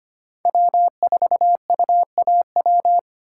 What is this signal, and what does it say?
Morse code, heard as a keyed tone, about 25 wpm, 700 Hz. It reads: W4UAW
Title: USA Call Signs - 5 Characters - 3x Character Spacing - 25wpm